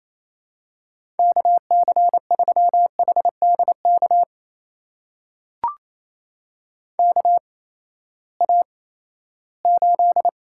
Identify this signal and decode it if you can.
Morse code, heard as a keyed tone, about 28 wpm, spaced as for 16 wpm, 700 Hz. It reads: KC3HDK KA8
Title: USA Call Signs - 6 Characters - 8x Character Spacing - 28wpm